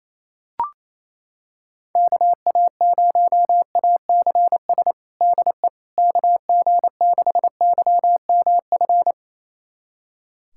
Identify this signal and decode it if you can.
Morse code, heard as a keyed tone, about 27 wpm, 700 Hz. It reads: KA0ACS DE KG6YMF
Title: USA Callsign DE Callsign 28wpm